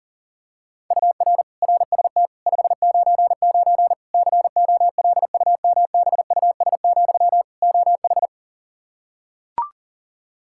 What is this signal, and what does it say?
Morse code, heard as a keyed tone, about 37 wpm, 700 Hz. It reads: UR RST 599 COLUMBUS, OH
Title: K1USN SST QSO Practice 40wpm